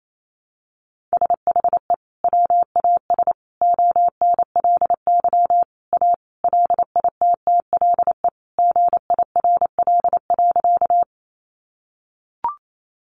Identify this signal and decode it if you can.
Morse code, heard as a keyed tone, about 28 wpm, 700 Hz. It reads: SHE WAS ONLY A LITTLE GIRL.
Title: Sentences from Top 400 Words 28wpm